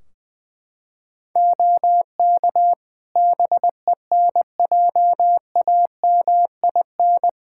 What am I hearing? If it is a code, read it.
Morse code, heard as a keyed tone, about 21 wpm, 700 Hz. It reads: OK BENJAMIN
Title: Practice QSO - Exchange 3 - More Info - 20wpm